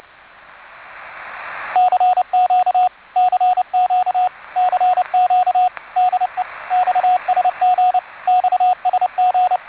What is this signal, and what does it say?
Morse code, heard as a keyed tone, about 29 wpm, 730 Hz, with radio noise. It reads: CQ CQ CQ DE XSG XSG